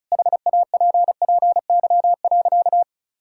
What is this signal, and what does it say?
Morse code, heard as a keyed tone, about 34 wpm, 700 Hz. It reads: HAPPY.
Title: Sentences from Top 1000 Words - Review Part 5 - 35wpm